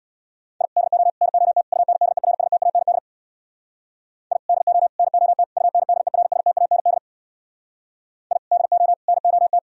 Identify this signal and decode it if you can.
Morse code, heard as a keyed tone, about 64 wpm, 700 Hz. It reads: I DO NOT UNDERSTAND I DO NOT UNDERSTAND I DO NOT